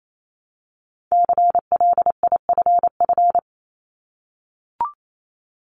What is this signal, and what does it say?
Morse code, heard as a keyed tone, about 28 wpm, 700 Hz. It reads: CLIFF
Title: Multi-letter Phonemes 28wpm